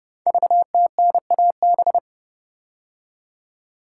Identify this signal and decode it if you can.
Morse code, heard as a keyed tone, about 30 wpm, 700 Hz. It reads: VTNAB
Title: Code Groups - 5 Letters - 30wpm